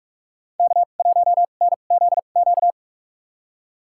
Morse code, heard as a keyed tone, about 45 wpm, spaced as for 32 wpm, 700 Hz. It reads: K1NZQ
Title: USA Call Signs - 5 Characters - 2x Character Spacing - 45wpm